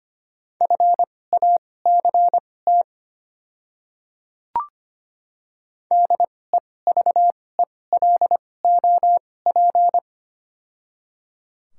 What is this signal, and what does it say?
Morse code, heard as a keyed tone, about 25 wpm, spaced as for 18 wpm, 700 Hz. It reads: FACT DEVELOP
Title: 2x Character Spacing - Top 500 Words - 25wpm